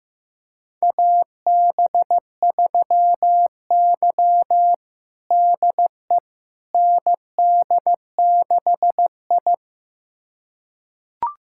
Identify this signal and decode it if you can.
Morse code, heard as a keyed tone, about 15 wpm, 700 Hz. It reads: AB3Y DE ND6I
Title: USA Callsign DE Callsign - 4 Characters - 15wpm